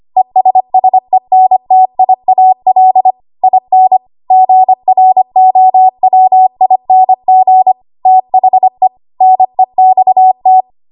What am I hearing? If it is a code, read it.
Morse code, heard as a keyed tone, about 25 wpm, 750 Hz. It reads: ESSENTIAL IN GROWING THE NEXT
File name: Archive / 25 WPM / 25WPM